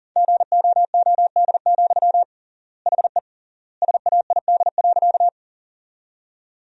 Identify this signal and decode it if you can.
Morse code, heard as a keyed tone, about 40 wpm, 700 Hz. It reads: GOOD, HE SAID.
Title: 3x Word-Spacing - Sentences from Top 200 Words - 40wpm